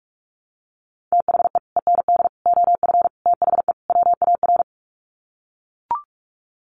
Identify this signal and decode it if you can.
Morse code, heard as a keyed tone, about 44 wpm, 700 Hz. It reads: THE END OF THE WAR
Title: Sets of 5 Words 45wpm